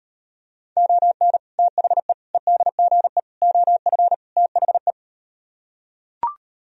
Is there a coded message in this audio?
Yes: Morse code from 0.8 to 4.9 s, heard as a keyed tone, about 39 wpm, 700 Hz. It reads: ON THE EDGE OF THE